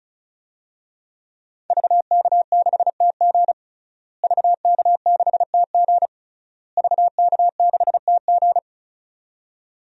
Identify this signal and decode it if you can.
Morse code, heard as a keyed tone, about 34 wpm, 700 Hz. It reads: VK6TG VK6TG VK6TG
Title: DXCC-100 Call Signs - 5 Characters - Repeated 3x - 35wpm